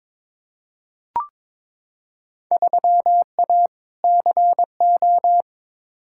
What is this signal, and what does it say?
Morse code, heard as a keyed tone, about 21 wpm, 700 Hz. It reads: 3A CO
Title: Field Day Exchanges Encore - 22wpm